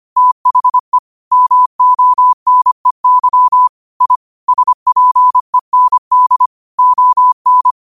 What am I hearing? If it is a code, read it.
Morse code, heard as a keyed tone, about 25 wpm, 1000 Hz. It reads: THE MONEY I SPEND ON
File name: qotdfiles / 25wpm